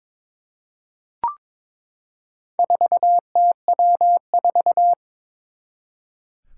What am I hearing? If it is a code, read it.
Morse code, heard as a keyed tone, about 22 wpm, 700 Hz. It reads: 4TW4
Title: Four Challenging Letter-Number Combinations 22wpm